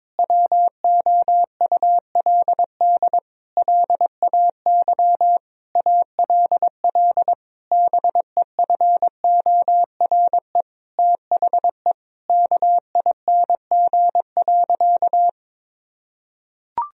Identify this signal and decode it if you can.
Morse code, heard as a keyed tone, about 22 wpm, 700 Hz. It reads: WOULD LAY ALL BEFORE THE KING.
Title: Sentences from Top 1000 Words - Review Part 2 - 22wpm